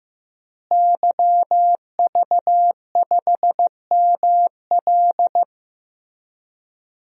Morse code, heard as a keyed tone, about 16 wpm, 700 Hz. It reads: YV5ML
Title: DXCC-100 Call Signs - 5 Characters - Repeated 3x - 15wpm